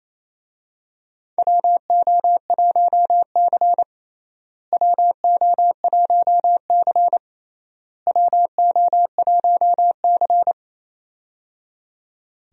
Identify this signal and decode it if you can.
Morse code, heard as a keyed tone, about 28 wpm, 700 Hz. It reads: WO1C WO1C WO1C
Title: USA Call Signs - 4 Characters - Repeated 3x - 28wpm